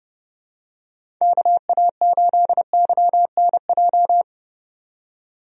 Morse code, heard as a keyed tone, about 30 wpm, 700 Hz. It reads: KA8YNJ
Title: USA Call Signs - 6 Characters - 30wpm